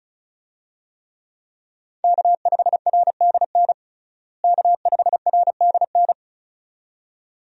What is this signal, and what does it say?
Morse code, heard as a keyed tone, about 34 wpm, 700 Hz. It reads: K5RDN K5RDN
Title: USA Call Signs - 5 Characters - Repeated 2x - 35wpm